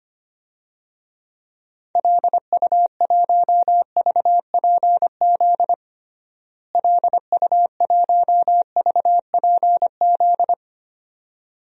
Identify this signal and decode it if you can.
Morse code, heard as a keyed tone, about 25 wpm, 700 Hz. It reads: LU1VPZ LU1VPZ
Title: DXCC-100 Call Signs - 6 Characters - Repeated 2x - 25wpm